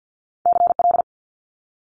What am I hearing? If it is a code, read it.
Morse code, heard as a keyed tone, about 50 wpm, 700 Hz. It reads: CL
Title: Multi-letter Phonemes Quick Fire 50wpm